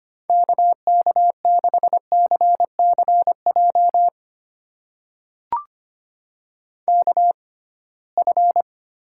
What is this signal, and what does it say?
Morse code, heard as a keyed tone, about 25 wpm, spaced as for 19 wpm, 700 Hz. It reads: KK6CCJ KF
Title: USA Call Signs - 6 Characters - 6x Character Spacing - 25wpm